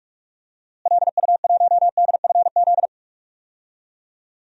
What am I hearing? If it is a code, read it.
Morse code, heard as a keyed tone, about 43 wpm, 700 Hz. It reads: LU1DFZ
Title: DXCC-100 Call Signs - 6 Characters - Repeated 2x - 45wpm